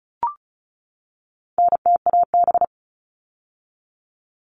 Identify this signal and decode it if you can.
Morse code, heard as a keyed tone, about 34 wpm, 700 Hz. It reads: NTAB
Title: All Cut Numbers - 4-Digit 35wpm